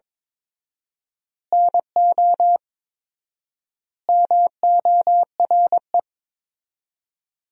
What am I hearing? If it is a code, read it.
Morse code, heard as a keyed tone, about 22 wpm, 700 Hz. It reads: NO MORE
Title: 4x Word-Spacing - Sentences from Top 200 Words - 22wpm